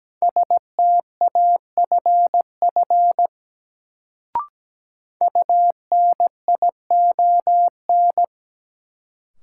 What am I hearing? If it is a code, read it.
Morse code, heard as a keyed tone, about 17 wpm, 700 Hz. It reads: STAFF UNION
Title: Sets of 5 Words Encore - New Word Drill - 17wpm